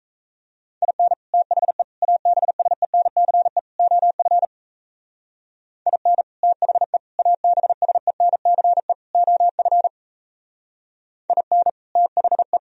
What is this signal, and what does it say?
Morse code, heard as a keyed tone, about 37 wpm, 700 Hz. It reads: IN THE ABSENCE OF IN THE ABSENCE OF IN THE